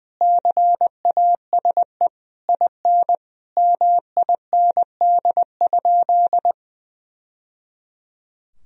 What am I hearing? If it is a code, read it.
Morse code, heard as a keyed tone, about 21 wpm, 700 Hz. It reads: CASE IN MIND?
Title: Sentences from Top 1000 Words - Review Part 4 - 20wpm